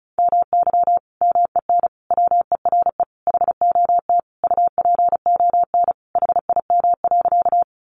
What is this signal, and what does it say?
Morse code, heard as a keyed tone, about 34 wpm, 700 Hz. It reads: MY MEN WERE HOT UPON HIM.
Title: Sentences from Top 1100 Words 35wpm